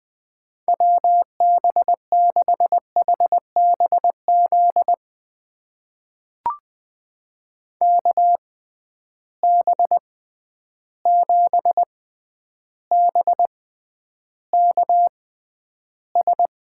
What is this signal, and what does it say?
Morse code, heard as a keyed tone, about 20 wpm, spaced as for 10 wpm, 700 Hz. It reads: WB6HBZ KB7BKS
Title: USA Call Signs - 6 Characters - 6x Character Spacing - 20wpm